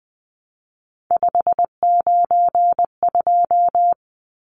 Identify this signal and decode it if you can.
Morse code, heard as a keyed tone, about 20 wpm, 700 Hz. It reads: 592
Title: Numbers - 3-Digit 20wpm